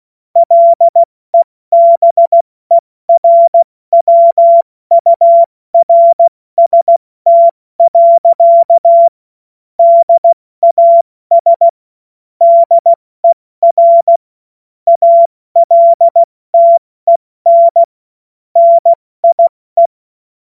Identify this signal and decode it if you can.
Morse code, heard as a keyed tone, about 16 wpm, spaced as for 14 wpm, 680 Hz. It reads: LEBERWURST. DAS DER ALTEN NIE
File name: Morsekurs / Witze / Witz 01. November 2022 70 F80